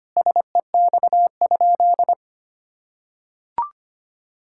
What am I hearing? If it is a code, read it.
Morse code, heard as a keyed tone, about 25 wpm, 700 Hz. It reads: SEX?
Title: Sentences from Top 1600 Words 25wpm